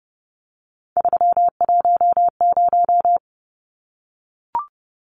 Morse code, heard as a keyed tone, about 30 wpm, 700 Hz. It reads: 310
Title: Numbers - 3-Digit 30wpm